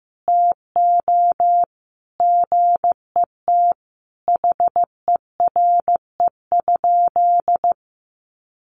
Morse code, heard as a keyed tone, about 15 wpm, 700 Hz. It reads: TO GET HERE?
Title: Sentences from Top 1400 Words 15wpm